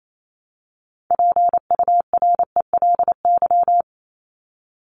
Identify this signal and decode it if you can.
Morse code, heard as a keyed tone, about 28 wpm, 700 Hz. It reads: PURELY
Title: LY Suffix - Top 100 Words with Base Words 28wpm